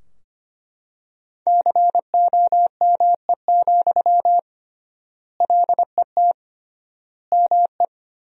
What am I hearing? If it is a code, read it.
Morse code, heard as a keyed tone, about 25 wpm, 700 Hz. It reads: COME, LET ME
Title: 3x Word-Spacing - Sentences from Top 500 Words - 25wpm